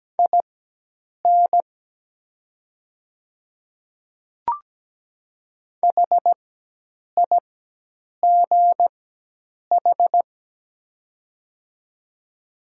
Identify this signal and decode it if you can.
Morse code, heard as a keyed tone, about 17 wpm, spaced as for 8 wpm, 700 Hz. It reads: IN HIGH
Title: 4x Character Spacing - Top 200 Words - 17wpm